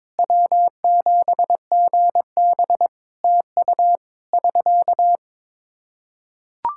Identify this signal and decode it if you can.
Morse code, heard as a keyed tone, about 22 wpm, 700 Hz. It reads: W7GB TU <SK>
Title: Practice QSO - Exchange 4 - Ending - 22wpm